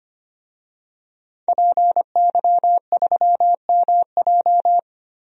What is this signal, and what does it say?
Morse code, heard as a keyed tone, about 25 wpm, 700 Hz. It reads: PY3MJ
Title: DXCC-100 Call Signs - 5 Characters - Repeated 2x - 25wpm